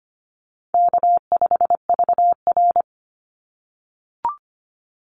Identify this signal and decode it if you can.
Morse code, heard as a keyed tone, about 25 wpm, 700 Hz. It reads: K5VR
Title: Easy Call Signs - 25wpm